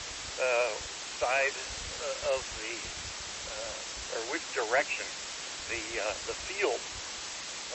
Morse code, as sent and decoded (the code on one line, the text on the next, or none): none